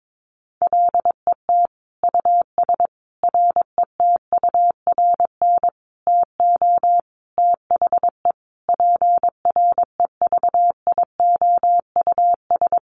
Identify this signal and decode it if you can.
Morse code, heard as a keyed tone, about 22 wpm, 700 Hz. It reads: LET US RETURN TO THE PREVIOUS
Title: Sentences from Top 1500 Words 22wpm